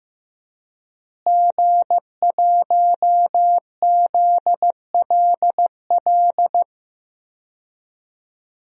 Morse code, heard as a keyed tone, about 15 wpm, 700 Hz. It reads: G1ZLL
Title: DXCC-100 Call Signs - 5 Characters - Repeated 3x - 15wpm